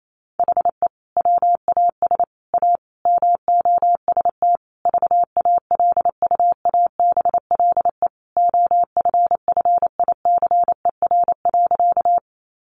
Morse code, heard as a keyed tone, about 28 wpm, 700 Hz. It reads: HE WAS A MOST VALUABLE OFFICER.
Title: Sentences from Top 1200 Words 28wpm